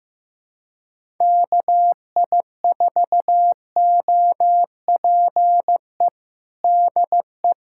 Morse code, heard as a keyed tone, about 15 wpm, 700 Hz. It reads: KI4OPE DE
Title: USA Callsign DE Callsign 15wpm